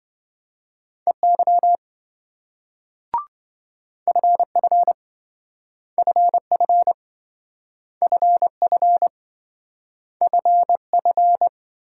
Morse code, heard as a keyed tone, about 27 wpm, 700 Hz. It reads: EY FF FF FF FF